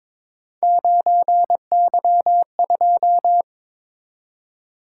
Morse code, heard as a keyed tone, about 22 wpm, 700 Hz. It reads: 9Y2
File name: practice / 22wpm / Three Letter-Number Combinations - 22wpm